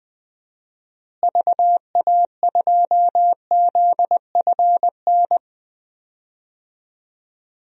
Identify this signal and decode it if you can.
Morse code, heard as a keyed tone, about 20 wpm, 700 Hz. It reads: VA2ZFN
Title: DXCC-100 Call Signs - 6 Characters - Repeated 3x - 20wpm